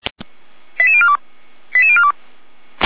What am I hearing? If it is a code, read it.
Five-tone selective call (ZVEI), heard as quick runs of five tones, tones of about 70 ms. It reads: 88042 88042
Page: /455/SDR/ZVEI/88074/